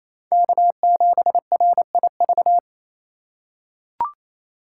Morse code, heard as a keyed tone, about 28 wpm, 700 Hz. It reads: K7RIV
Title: USA Call Signs - 5 Characters - 4x Character Spacing - 28wpm